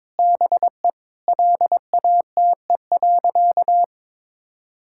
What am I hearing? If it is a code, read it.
Morse code, heard as a keyed tone, about 22 wpm, 700 Hz. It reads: BE LATE.
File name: practice / 22wpm / Sentences with 4-Word Phrases - 22wpm